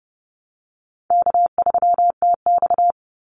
Morse code, heard as a keyed tone, about 30 wpm, 700 Hz. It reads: K3TX
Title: Easy Call Signs Repeated 2x - 30wpm